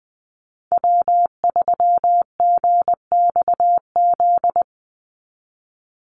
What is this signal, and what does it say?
Morse code, heard as a keyed tone, about 20 wpm, 700 Hz. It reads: W3GXZ
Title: USA Call Signs 20wpm